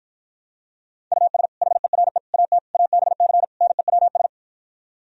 Morse code, heard as a keyed tone, about 52 wpm, 700 Hz. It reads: US HERE AT ABC NEWS